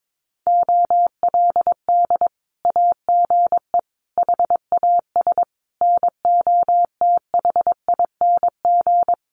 Morse code, heard as a keyed tone, about 22 wpm, 700 Hz. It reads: OLD AGE HAS NOTHING
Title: Sentences from Top 1100 Words 22wpm